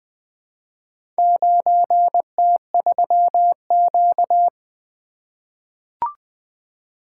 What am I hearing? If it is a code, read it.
Morse code, heard as a keyed tone, about 20 wpm, 700 Hz. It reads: 9T3Q
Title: Four Letter-Number Combinations - 20wpm